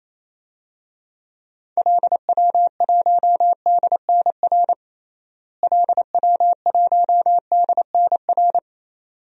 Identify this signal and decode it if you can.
Morse code, heard as a keyed tone, about 28 wpm, 700 Hz. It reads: LW1DNR LW1DNR
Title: DXCC-100 Call Signs - 6 Characters - Repeated 2x - 28wpm